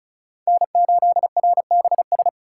Morse code, heard as a keyed tone, about 34 wpm, 700 Hz. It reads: N8RBS
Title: USA Call Signs - 4x Character Spacing - 35wpm